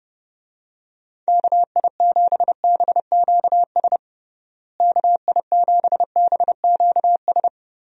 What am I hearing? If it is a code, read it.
Morse code, heard as a keyed tone, about 32 wpm, 700 Hz. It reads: KI7BQS KI7BQS
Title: USA Call Signs - 6 Characters - Repeated 3x - 30wpm